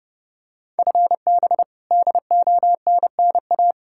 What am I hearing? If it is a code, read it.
Morse code, heard as a keyed tone, about 32 wpm, 700 Hz. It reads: FB DONNA
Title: Practice QSO - Exchange 3 - More Info - 30wpm